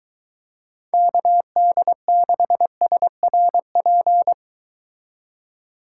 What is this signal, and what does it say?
Morse code, heard as a keyed tone, about 23 wpm, 700 Hz. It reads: KD6SRP